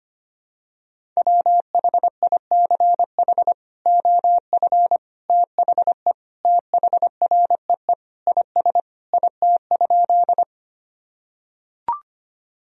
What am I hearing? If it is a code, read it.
Morse code, heard as a keyed tone, about 25 wpm, 700 Hz. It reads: WHICH OF THE THREE IS IT?
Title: Sentences from Top 1000 Words - Review Part 1 - 25wpm